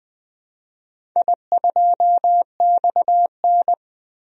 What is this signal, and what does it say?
Morse code, heard as a keyed tone, about 20 wpm, 700 Hz. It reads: I2XN
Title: DXCC-100 Call Signs - 4 Characters - 20wpm